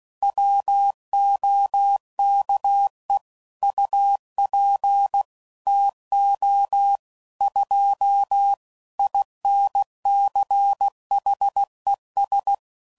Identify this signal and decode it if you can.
Morse code, heard as a keyed tone, about 16 wpm, 800 Hz. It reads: WOKE UP TO 2 INCHES
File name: CW Code Groups @15WPM / Weather Sets / Weather 3 17